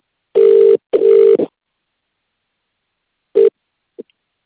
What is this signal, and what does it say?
Ringback tone (UK), 400 + 450 Hz.